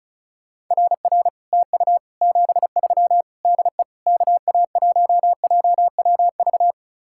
Morse code, heard as a keyed tone, about 42 wpm, 700 Hz. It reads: RR TU 73 DE KA1JWV
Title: Speed-Racing Common QSO Phrases Encore - 35wpm